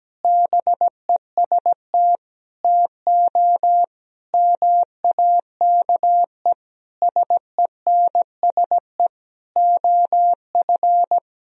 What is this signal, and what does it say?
Morse code, heard as a keyed tone, about 17 wpm, 700 Hz. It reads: BEST TO MAKE SENSE OF
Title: Sentences from Top 1000 Words - Review Part 4 - 17wpm